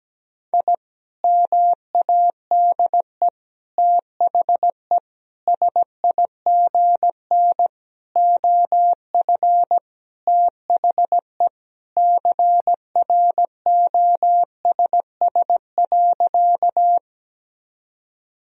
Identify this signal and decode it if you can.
Morse code, heard as a keyed tone, about 17 wpm, 700 Hz. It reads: I MADE THE SIGN OF THE CROSS.
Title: Sentences from Top 1000 Words - Review Part 1 - 17wpm